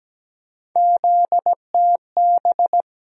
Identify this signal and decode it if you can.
Morse code, heard as a keyed tone, about 17 wpm, 700 Hz. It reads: ZTB
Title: USA Call Sign Postfixes - 3 Letters 17wpm